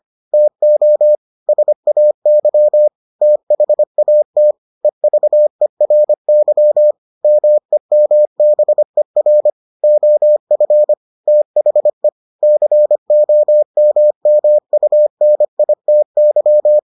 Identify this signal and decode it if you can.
Morse code, heard as a keyed tone, about 25 wpm, 600 Hz. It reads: TO SAY THAT EVERY MEMBER OF THE COMMUNITY